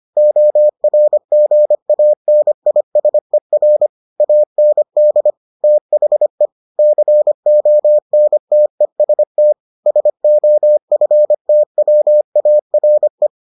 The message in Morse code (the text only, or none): ORGANISER AND THE CONTEST SOFTWARE